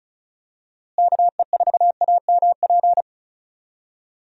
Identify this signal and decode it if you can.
Morse code, heard as a keyed tone, about 36 wpm, 700 Hz. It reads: KE4AMP
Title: USA Call Signs - 6 Characters - Repeated 2x - 35wpm